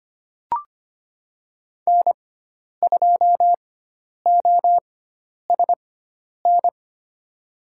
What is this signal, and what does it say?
Morse code, heard as a keyed tone, about 25 wpm, spaced as for 10 wpm, 700 Hz. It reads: N2OSN
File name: Lesson 16g - Letter c - Call Signs 25wpm